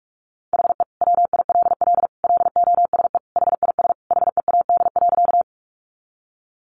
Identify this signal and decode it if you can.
Morse code, heard as a keyed tone, about 44 wpm, 700 Hz. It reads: HE WILL LOSE HIS HEAD.
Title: Sentences from Top 1100 Words 45wpm